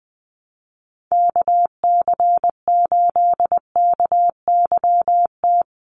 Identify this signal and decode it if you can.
Morse code, heard as a keyed tone, about 20 wpm, 700 Hz. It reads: KC8KYT
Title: USA Call Signs 20wpm